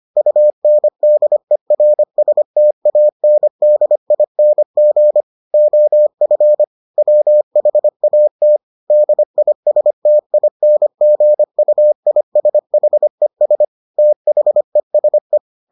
Morse code, heard as a keyed tone, about 25 wpm, 600 Hz. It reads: UNDERSTANDING OF WHAT DISTINGUISHES THESE